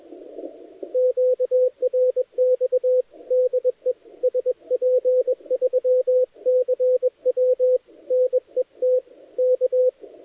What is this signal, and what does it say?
Morse code, heard as a keyed tone, about 21 wpm, 510 Hz, with radio noise. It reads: QRX DE SP3CW NET K
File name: dlazaawansowanych / nagraniazpasma / titawaNETcontrol